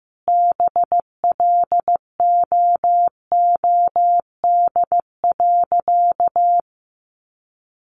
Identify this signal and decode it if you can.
Morse code, heard as a keyed tone, about 15 wpm, 700 Hz. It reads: BLOOD.
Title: Sentences from Top 1100 Words 15wpm